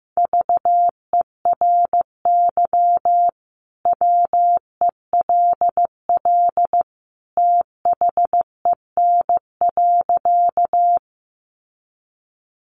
Morse code, heard as a keyed tone, about 15 wpm, 700 Hz. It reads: VERY WELL THEN.
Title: Sentences from Top 200 Words 15wpm